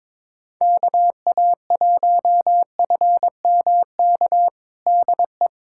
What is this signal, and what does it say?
Morse code, heard as a keyed tone, about 22 wpm, 700 Hz. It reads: KA1FMK DE
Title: USA Callsign DE Callsign - 6 Characters - 22wpm